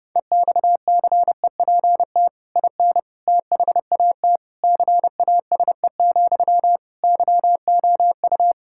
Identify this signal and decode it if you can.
Morse code, heard as a keyed tone, about 30 wpm, 700 Hz. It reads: EXCEPT IN THAT CASE, YOU
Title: Sentences from Top 1600 Words 30wpm